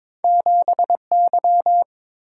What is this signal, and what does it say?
Morse code, heard as a keyed tone, about 23 wpm, 700 Hz. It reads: 7Y
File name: practice / 22wpm / Two Letter-Number Combinations - 22wpm